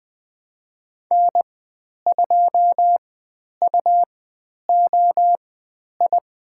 Morse code, heard as a keyed tone, about 20 wpm, spaced as for 10 wpm, 700 Hz. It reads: N2UOI
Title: 102 Lesson 17g - Letter u - Call Signs 20wpm